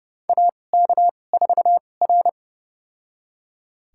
Morse code, heard as a keyed tone, about 30 wpm, spaced as for 22 wpm, 700 Hz. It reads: AK4R
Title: USA Call Signs - 4 Characters - 2x Character Spacing - 30wpm